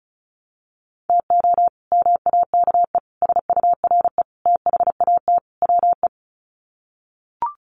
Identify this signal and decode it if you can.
Morse code, heard as a keyed tone, about 34 wpm, 700 Hz. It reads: TO MAKE SURE THAT WE
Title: Sets of 5 Words Encore 35wpm